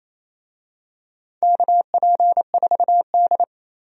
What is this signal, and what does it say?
Morse code, heard as a keyed tone, about 28 wpm, 700 Hz. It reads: KP4D
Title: DXCC-100 Call Signs - 4 Characters - 28wpm